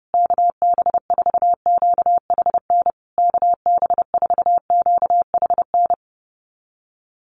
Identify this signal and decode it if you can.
Morse code, heard as a keyed tone, about 30 wpm, 700 Hz. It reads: KB4QHN KB4QHN
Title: USA Call Signs Repeated 3x 30wpm